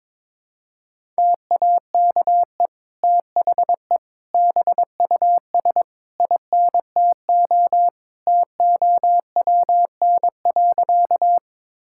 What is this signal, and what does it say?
Morse code, heard as a keyed tone, about 22 wpm, 700 Hz. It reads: TAKE THE BUS INTO TOWN.
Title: Sentences from Top 1000 Words - Review Part 6 - 22wpm